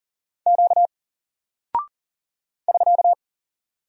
Morse code, heard as a keyed tone, about 39 wpm, 700 Hz. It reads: Q <SK>
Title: Single Character (Letter:Number:Punctuation:Prosign) 40wpm